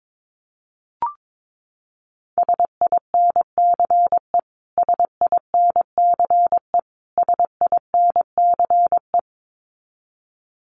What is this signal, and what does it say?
Morse code, heard as a keyed tone, about 22 wpm, 700 Hz. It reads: SINCE SINCE SINCE
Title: Top 500 Words Repeated 22wpm